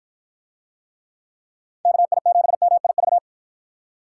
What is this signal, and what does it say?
Morse code, heard as a keyed tone, about 53 wpm, 700 Hz. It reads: KI7MIV